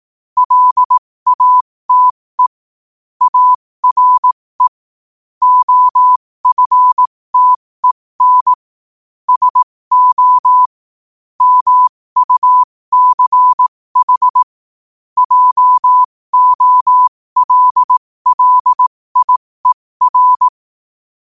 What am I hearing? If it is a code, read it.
Morse code, heard as a keyed tone, about 18 wpm, spaced as for 15 wpm, 1000 Hz. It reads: LATE ARE OFTEN SO MUCH JOLLIER
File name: qotdfiles / 15wpm